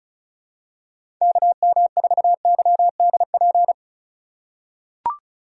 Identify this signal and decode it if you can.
Morse code, heard as a keyed tone, about 36 wpm, 700 Hz. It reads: KM4YDP
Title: USA Call Signs - 6 Characters - 2x Character Spacing - 35wpm